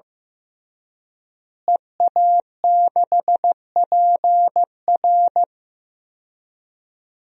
Morse code, heard as a keyed tone, about 16 wpm, 700 Hz. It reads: EA6PR
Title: DXCC-100 Call Signs - 5 Characters - Repeated 3x - 15wpm